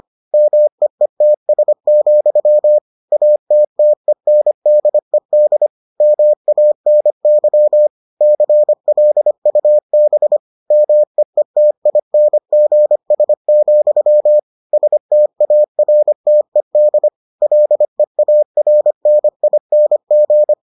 Morse code, heard as a keyed tone, about 25 wpm, 600 Hz. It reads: MEETS, ATTENDED MANY CLUB MEETINGS, STARTED LEARNING